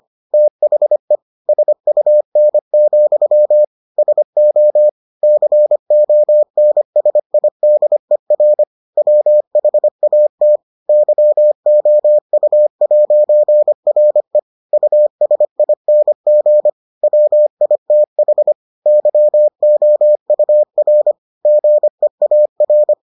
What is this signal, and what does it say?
Morse code, heard as a keyed tone, about 25 wpm, 600 Hz. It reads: THE SUN, SO CONSIDER WHAT YOU'RE USING WITH YOUR GEAR